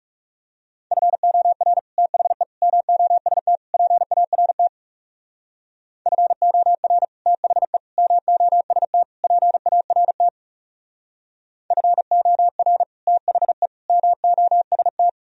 Morse code, heard as a keyed tone, about 36 wpm, 700 Hz. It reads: FOR THE MOST PART FOR THE MOST PART FOR THE MOST
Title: Speed-Racing - Sets of 4 Words 30wpm